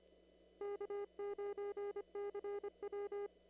Morse code, heard as a keyed tone, about 25 wpm, 410 Hz. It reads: K9CW
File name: CW / kt5j wpx cw-op-k5pi